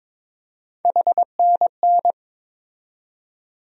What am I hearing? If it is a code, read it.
Morse code, heard as a keyed tone, about 22 wpm, 700 Hz. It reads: HNN
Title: USA Call Sign Postfixes - 3 Letters 22wpm